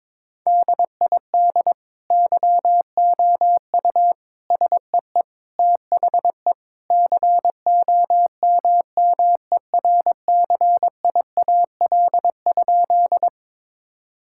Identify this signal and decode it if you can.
Morse code, heard as a keyed tone, about 22 wpm, 700 Hz. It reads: DID YOU SEE THE COMMERCIAL?
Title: Sentences from Top 1000 Words - Review Part 5 - 22wpm